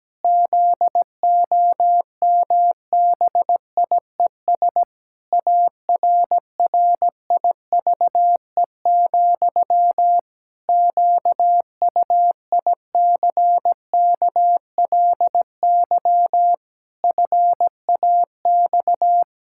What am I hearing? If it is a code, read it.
Morse code, heard as a keyed tone, about 17 wpm, 700 Hz. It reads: ZOMBIES ARRIVE, QUICKLY FAX
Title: Pangram Challenge v2 - 17wpm